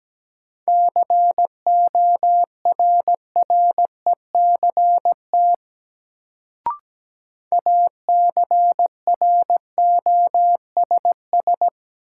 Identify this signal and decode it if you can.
Morse code, heard as a keyed tone, about 17 wpm, 700 Hz. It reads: CORRECT ACROSS
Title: Sentences from Top 700 Words - New Word Drill - 17wpm